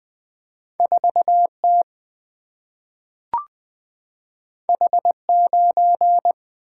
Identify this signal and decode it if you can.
Morse code, heard as a keyed tone, about 20 wpm, 700 Hz. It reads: 4T H9
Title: Single-Letter Single-Number 20wpm